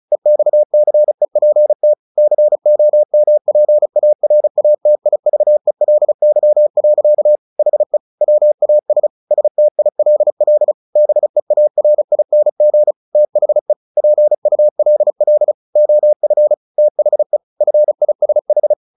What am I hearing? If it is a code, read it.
Morse code, heard as a keyed tone, about 36 wpm, 600 Hz. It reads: EXCEPT COMPARATIVELY. HE WAS STILL BEARING THE PULL OF THE FISH
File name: the old man and the sea 2 35wpm